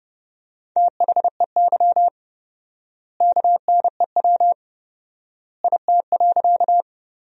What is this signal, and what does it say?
Morse code, heard as a keyed tone, about 30 wpm, 700 Hz. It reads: THEY KNEW IT.
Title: 4x Word-Spacing - Sentences from Top 400 Words - 30wpm